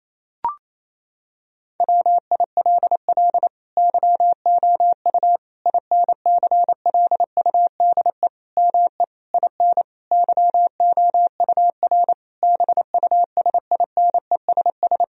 Morse code, heard as a keyed tone, about 28 wpm, 700 Hz. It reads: WILL YOU INCLUDE ME IN YOUR BUSINESS
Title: Sentences from Top 1000 Words - Review Part 3 - 28wpm